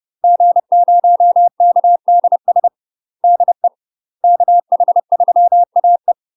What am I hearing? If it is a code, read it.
Morse code, heard as a keyed tone, about 30 wpm, 700 Hz. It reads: G0KDS DE KH3AE